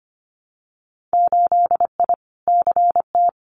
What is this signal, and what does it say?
Morse code, heard as a keyed tone, about 25 wpm, 690 Hz. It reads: 8I CT
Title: Winter Field Day Exchanges Encore 25wpm